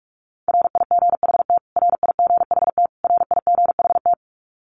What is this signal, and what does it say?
Morse code, heard as a keyed tone, about 44 wpm, 700 Hz. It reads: RIGHT RIGHT RIGHT
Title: Top 500 Words Repeated 45wpm